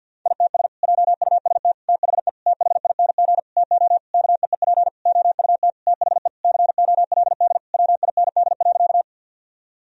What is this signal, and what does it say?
Morse code, heard as a keyed tone, about 50 wpm, 700 Hz. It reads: ITS JUST THE THING TO KEEP OUT THE COLD WIND.
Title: Sentences from Top 1000 Words - Review Part 2 - 50wpm